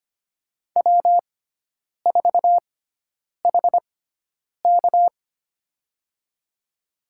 Morse code, heard as a keyed tone, about 25 wpm, spaced as for 9 wpm, 700 Hz. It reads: W4HK
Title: USA Call Signs - 6x Character Spacing - 25wpm